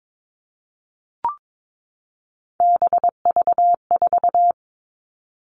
Morse code, heard as a keyed tone, about 22 wpm, 700 Hz. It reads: BV4
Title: All Cut Numbers - 3-Digit 22wpm